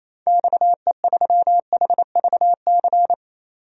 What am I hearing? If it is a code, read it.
Morse code, heard as a keyed tone, about 28 wpm, 700 Hz. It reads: XE3HVC
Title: DXCC-100 Call Signs - 6 Characters - 28wpm